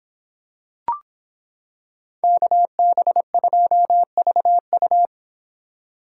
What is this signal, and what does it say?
Morse code, heard as a keyed tone, about 26 wpm, 700 Hz. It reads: KB2VU